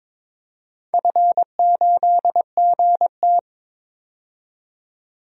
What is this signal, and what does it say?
Morse code, heard as a keyed tone, about 22 wpm, 700 Hz. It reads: F8GT
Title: DXCC-100 Call Signs - 4 Characters - 22wpm